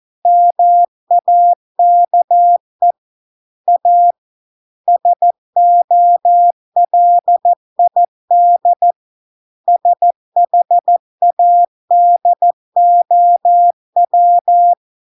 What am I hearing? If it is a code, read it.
Morse code, heard as a keyed tone, about 14 wpm, 700 Hz. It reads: MAKE A SOLID SHADOW